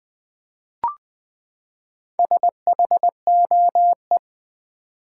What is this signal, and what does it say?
Morse code, heard as a keyed tone, about 20 wpm, 700 Hz. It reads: SHOE
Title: Four Letter Words - 20wpm